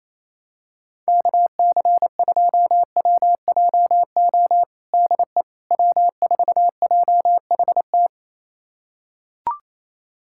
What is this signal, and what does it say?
Morse code, heard as a keyed tone, about 28 wpm, 700 Hz. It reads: KC2WJO DE W4JHT
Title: USA Callsign DE Callsign 28wpm